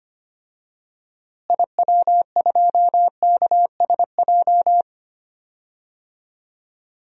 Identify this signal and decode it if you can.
Morse code, heard as a keyed tone, about 25 wpm, 700 Hz. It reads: IW2KSJ
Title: DXCC-100 Call Signs - 6 Characters - Repeated 2x - 25wpm